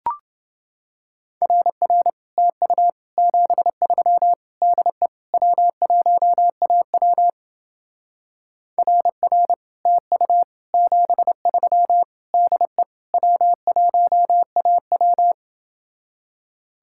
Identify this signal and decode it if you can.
Morse code, heard as a keyed tone, about 27 wpm, 700 Hz. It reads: RR TU 73 DE W1AW RR TU 73 DE W1AW